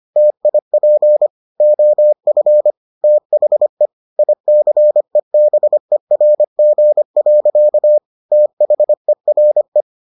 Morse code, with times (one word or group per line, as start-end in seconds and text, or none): 0.2-1.3 TIP
1.6-2.7 OF
3.0-3.9 THE
4.2-8.0 ICEBERG.
8.3-9.8 THERE